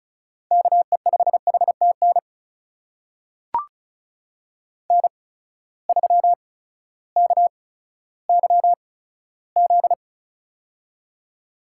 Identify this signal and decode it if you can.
Morse code, heard as a keyed tone, about 35 wpm, spaced as for 16 wpm, 700 Hz. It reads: KE5HTN N3KYZ
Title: USA Call Signs - 8x Character Spacing - 35wpm